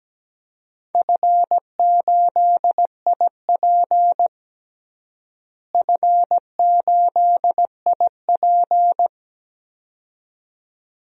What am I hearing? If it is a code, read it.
Morse code, heard as a keyed tone, about 17 wpm, 700 Hz. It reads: F8IP F8IP
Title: DXCC-100 Call Signs - 4 Characters - Repeated 2x - 17wpm